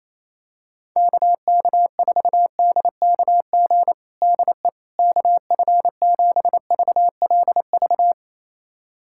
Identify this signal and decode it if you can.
Morse code, heard as a keyed tone, about 28 wpm, 700 Hz. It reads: KK4DKG DE KF7VLV
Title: USA Callsign DE Callsign 28wpm